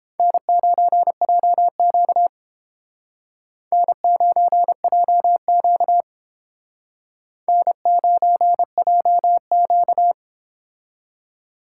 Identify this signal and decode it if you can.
Morse code, heard as a keyed tone, about 31 wpm, 700 Hz. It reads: N9JQ N9JQ N9JQ